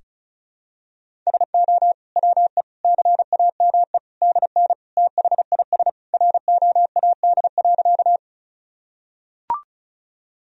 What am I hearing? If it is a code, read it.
Morse code, heard as a keyed tone, about 36 wpm, 700 Hz. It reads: SO WE CAME DN THIS ROAD.
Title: Sentences from Top 400 Words and Abbreviations 35wpm